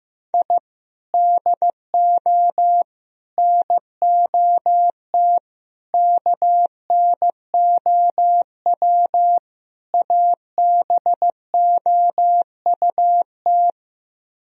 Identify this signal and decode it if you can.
Morse code, heard as a keyed tone, about 15 wpm, 700 Hz. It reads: I DO NOT KNOW ABOUT
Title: Speed-Racing - Sets of 5 Words 15wpm